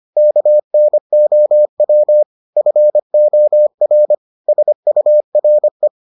Morse code, heard as a keyed tone, about 25 wpm, 600 Hz. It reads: KNOW FOR SURE